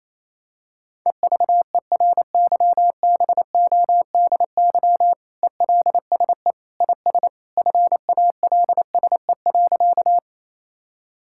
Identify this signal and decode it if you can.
Morse code, heard as a keyed tone, about 28 wpm, 700 Hz. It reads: EVERYBODY ELSE IS FALSE.
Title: Sentences from Top 1600 Words 28wpm